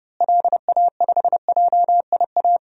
Morse code, heard as a keyed tone, about 30 wpm, 700 Hz. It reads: LA5JIA
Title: DXCC-100 Call Signs - 6 Characters - Repeated 2x - 30wpm